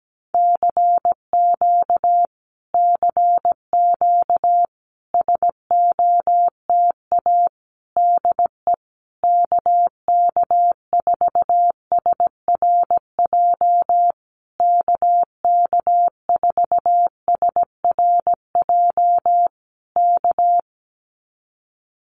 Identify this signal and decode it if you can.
Morse code, heard as a keyed tone, about 17 wpm, 700 Hz. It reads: CQ CQ SOTA DE KK4SRJ KK4SRJ K
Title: Common QSO Phrases Encore 17wpm